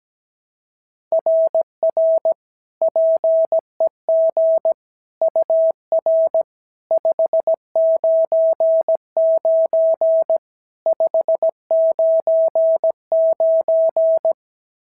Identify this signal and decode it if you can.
Morse code, heard as a keyed tone, about 16 wpm, 650 Hz. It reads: RR PEG UR 599 599
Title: SOTA QSO Practice 17wpm